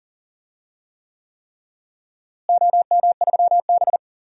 Morse code, heard as a keyed tone, about 40 wpm, 700 Hz. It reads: OM3B
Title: DXCC-100 Call Signs - 4 Characters - Repeated 3x - 40wpm